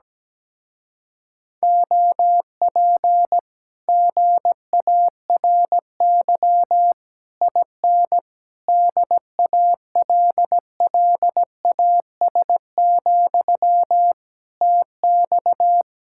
Morse code, heard as a keyed tone, about 17 wpm, 700 Hz. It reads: OP GARY IN DALLAS, TX
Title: Speed-Racing Common QSO Phrases - 17wpm